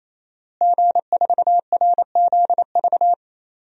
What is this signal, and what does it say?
Morse code, heard as a keyed tone, about 28 wpm, 690 Hz. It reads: G4RZV
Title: DXCC-100 Call Signs - 5 Characters - Repeated 3x - 28wpm